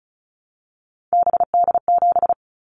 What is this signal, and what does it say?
Morse code, heard as a keyed tone, about 36 wpm, 700 Hz. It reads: BD7
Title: Callsign Prefixes for Top 100 DXCC - 35wpm